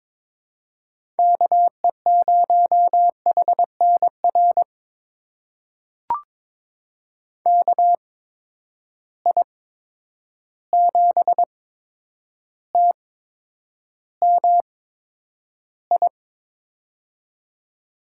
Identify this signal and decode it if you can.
Morse code, heard as a keyed tone, about 21 wpm, spaced as for 9 wpm, 700 Hz. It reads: KE0HNR KI7TMI
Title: USA Call Signs - 6 Characters - 8x Character Spacing - 22wpm